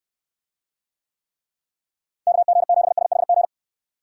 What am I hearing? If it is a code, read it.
Morse code, heard as a keyed tone, about 69 wpm, 700 Hz. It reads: KG7SHG